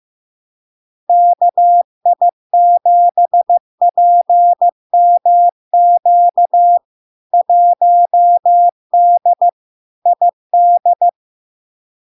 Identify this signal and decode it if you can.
Morse code, heard as a keyed tone, about 15 wpm, 700 Hz. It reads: KI7PMQ 1D ID
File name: LARK-FD-2022-Log-in-CW-15wpm-700-Hz